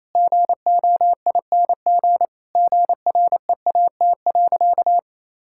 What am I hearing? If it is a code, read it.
Morse code, heard as a keyed tone, about 28 wpm, 700 Hz. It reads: GOING GREAT.
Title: Sentences from Top 1000 Words - Review Part 5 - 28wpm